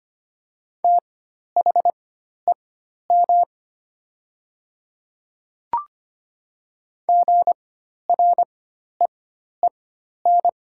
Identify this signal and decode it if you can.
Morse code, heard as a keyed tone, about 25 wpm, spaced as for 12 wpm, 700 Hz. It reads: THEM GREEN
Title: 4x Character Spacing - Top 500 Words - 25wpm